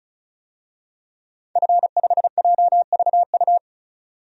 Morse code, heard as a keyed tone, about 36 wpm, 700 Hz. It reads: F5JVU
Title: DXCC-100 Call Signs - 5 Characters - 35wpm